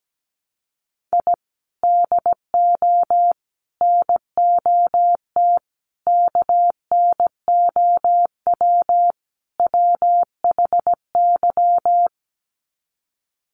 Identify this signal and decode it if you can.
Morse code, heard as a keyed tone, about 17 wpm, 700 Hz. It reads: I DO NOT KNOW WHY
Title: Sets of 5 Words 17wpm